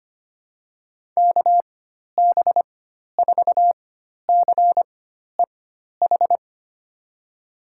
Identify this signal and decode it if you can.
Morse code, heard as a keyed tone, about 25 wpm, spaced as for 12 wpm, 700 Hz. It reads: KB4CEH
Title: USA Call Signs - 6 Characters - 4x Character Spacing - 25wpm